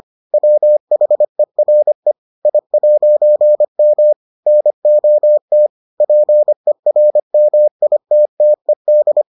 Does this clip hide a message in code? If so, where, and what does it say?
Yes: Morse code from 0.3 to 9.2 s, heard as a keyed tone, about 25 wpm, 600 Hz. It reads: WHERE I'M NOT PERMITTED